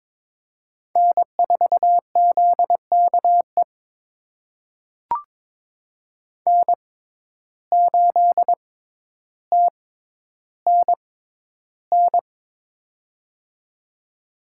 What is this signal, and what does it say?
Morse code, heard as a keyed tone, about 22 wpm, spaced as for 11 wpm, 700 Hz. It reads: N4ZKE N8TNN
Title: USA Call Signs - 5 Characters - 6x Character Spacing - 22wpm